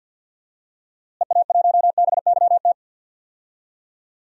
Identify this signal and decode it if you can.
Morse code, heard as a keyed tone, about 46 wpm, 700 Hz. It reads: EA1BYT
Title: DXCC-100 Call Signs - 6 Characters - Repeated 3x - 50wpm